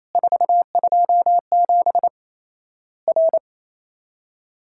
Morse code, heard as a keyed tone, about 28 wpm, 700 Hz. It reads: 427 R
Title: Worked All Europe DX Contest - QTC Exchange - 28wpm